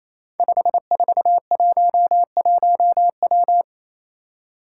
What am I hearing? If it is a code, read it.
Morse code, heard as a keyed tone, about 27 wpm, 700 Hz. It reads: 5411W
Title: Five Challenging Letter-Number Combinations 28wpm